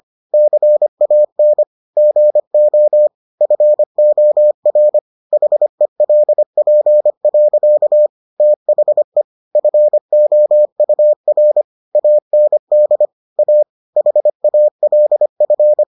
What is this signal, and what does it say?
Morse code, heard as a keyed tone, about 25 wpm, 600 Hz. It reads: CAN GO FOR HELP. THE FOUR AND A HALF